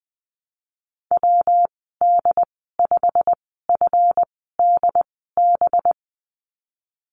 Morse code, heard as a keyed tone, about 20 wpm, spaced as for 15 wpm, 700 Hz. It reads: WD5FDB
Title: 186 Lesson 30g - Letter b - Call Signs 20wpm